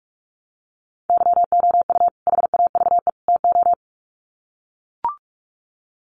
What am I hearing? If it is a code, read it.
Morse code, heard as a keyed tone, about 44 wpm, 700 Hz. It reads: YOU HAVE TO
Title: Sets of 3 Words 45wpm